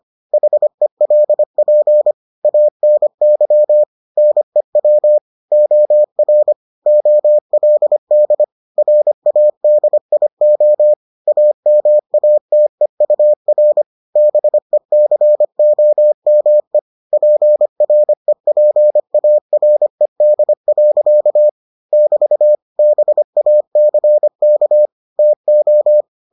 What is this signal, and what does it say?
Morse code, heard as a keyed tone, about 25 wpm, 600 Hz. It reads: HELP ANY NEW OR OLD RADIO AMATEUR BECOME PREPARED. <BT> BACK TO